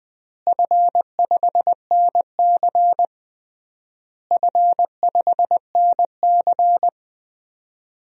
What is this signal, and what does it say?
Morse code, heard as a keyed tone, about 20 wpm, 700 Hz. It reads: F5NC F5NC
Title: DXCC-100 Call Signs - 4 Characters - Repeated 3x - 20wpm